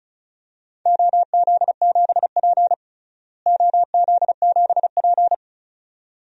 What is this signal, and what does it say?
Morse code, heard as a keyed tone, about 36 wpm, 700 Hz. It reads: OZ7P OZ7P
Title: DXCC-100 Call Signs - 4 Characters - Repeated 2x - 35wpm